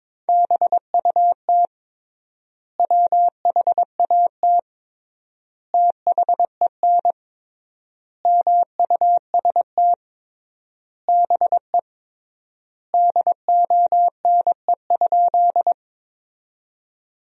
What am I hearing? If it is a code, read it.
Morse code, heard as a keyed tone, about 21 wpm, 700 Hz. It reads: BUT WHAT THEN MUST BE DONE?
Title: 3x Word-Spacing - Sentences from Top 500 Words - 22wpm